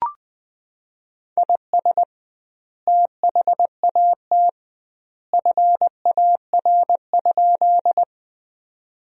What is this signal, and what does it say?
Morse code, heard as a keyed tone, about 20 wpm, 700 Hz. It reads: IS THAT FAR?
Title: 2x Word-Spacing - Sentences from Top 300 Words - 20wpm